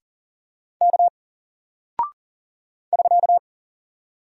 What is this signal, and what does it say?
Morse code, heard as a keyed tone, about 40 wpm, 700 Hz. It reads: K <SK>
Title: Single Character (Letter:Number:Punctuation:Prosign) 40wpm